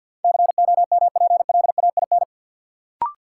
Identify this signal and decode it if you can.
Morse code, heard as a keyed tone, about 52 wpm, 700 Hz. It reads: COMPLAIN
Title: Sentences from Top 1300 Words - New Word Drill Alt - 50wpm